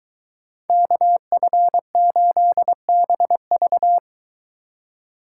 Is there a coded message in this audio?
Yes: Morse code from 0.7 to 4.0 s, heard as a keyed tone, about 23 wpm, 700 Hz. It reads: KF8BV